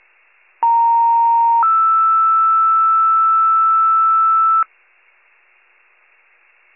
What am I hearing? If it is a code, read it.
Two-tone page, heard as a one-second tone, then a three-second tone, about 930 then 1320 Hz. It is heard from 0.6 to 4.6 s.
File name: wcalert